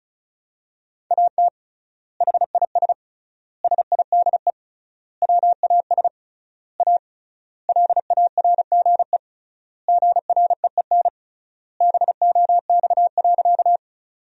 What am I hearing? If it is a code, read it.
Morse code, heard as a keyed tone, about 34 wpm, 700 Hz. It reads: AT HIS SIDE WAS A LARGE GREEN BOX.
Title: 3x Word-Spacing - Sentences from Top 500 Words - 35wpm